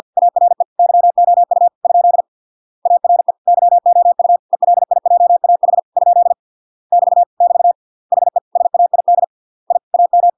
Morse code, heard as a keyed tone, about 50 wpm, 700 Hz. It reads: ARE YOU ? ARE YOU ELIJAH ? <BT> <BT> HE SAID I AM